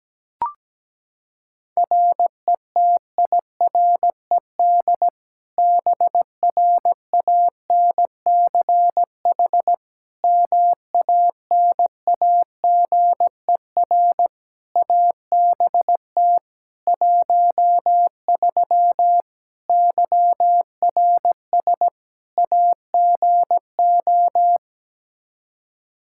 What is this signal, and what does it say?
Morse code, heard as a keyed tone, about 17 wpm, 700 Hz. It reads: RETIRED BRANCH MANAGER ABT 13 YRS AGO
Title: K1USN SST QSO Practice 17wpm